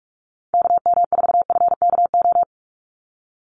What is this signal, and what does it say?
Morse code, heard as a keyed tone, about 44 wpm, 700 Hz. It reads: KM4FKO
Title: USA Call Signs 45wpm